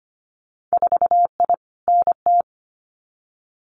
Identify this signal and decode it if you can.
Morse code, heard as a keyed tone, about 25 wpm, 690 Hz. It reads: 4I NT
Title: Winter Field Day Exchanges Encore 25wpm